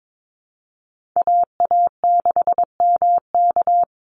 Morse code, heard as a keyed tone, about 22 wpm, 700 Hz. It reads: AA6MK
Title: Call Signs Repeated 2x - 22wpm